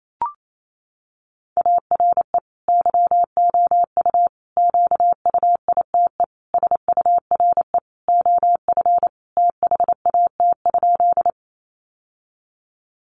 Morse code, heard as a keyed tone, about 28 wpm, 700 Hz. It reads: ARE YOU QUITE SURE OF THAT?
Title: Sentences from Top 700 Words 28wpm